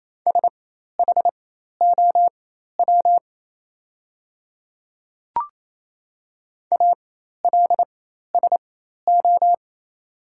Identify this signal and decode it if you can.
Morse code, heard as a keyed tone, about 28 wpm, spaced as for 13 wpm, 700 Hz. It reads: SHOW ALSO
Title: 4x Character Spacing - Top 200 Words - 28wpm